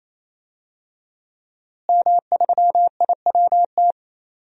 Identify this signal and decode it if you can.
Morse code, heard as a keyed tone, about 28 wpm, 700 Hz. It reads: M3IWT
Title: DXCC-100 Call Signs - 5 Characters - Repeated 3x - 28wpm